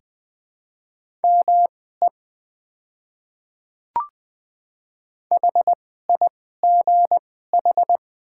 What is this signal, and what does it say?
Morse code, heard as a keyed tone, about 20 wpm, spaced as for 14 wpm, 700 Hz. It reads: ME HIGH
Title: 2x Character Spacing - Top 300 Words - 20wpm